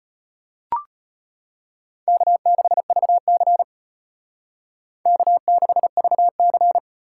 Morse code, heard as a keyed tone, about 31 wpm, 700 Hz. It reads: K6VC K6VC
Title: Speed Racing - USA 1x2 Call Signs 25wpm